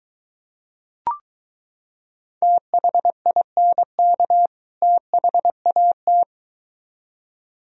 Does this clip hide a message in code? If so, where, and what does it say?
Yes: Morse code from 2.4 to 6.2 s, heard as a keyed tone, about 23 wpm, 700 Hz. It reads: THINK THAT